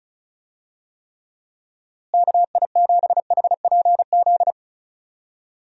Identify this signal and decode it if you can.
Morse code, heard as a keyed tone, about 34 wpm, 700 Hz. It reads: KI7HPZ
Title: USA Call Signs - 6 Characters - Repeated 2x - 35wpm